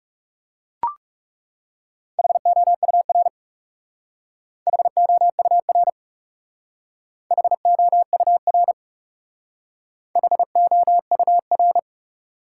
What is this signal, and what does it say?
Morse code, heard as a keyed tone, about 36 wpm, 700 Hz. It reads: HOUR HOUR HOUR HOUR
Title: Speed-Racing Top 180 QSO Elements 30wpm